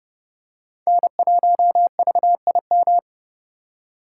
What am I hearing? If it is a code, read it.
Morse code, heard as a keyed tone, about 30 wpm, 700 Hz. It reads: N1VIM
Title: USA Call Signs - 5 Characters - 3x Character Spacing - 30wpm